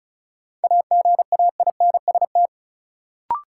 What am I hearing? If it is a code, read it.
Morse code, heard as a keyed tone, about 36 wpm, 700 Hz. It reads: AGAINST
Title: Sets of 2 Words - New Word Drill - 35wpm